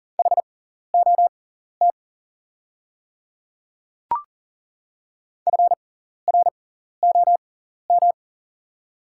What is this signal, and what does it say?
Morse code, heard as a keyed tone, about 40 wpm, spaced as for 14 wpm, 700 Hz. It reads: HOT FROM
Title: 6x Character Spacing - Top 100 Words - 40wpm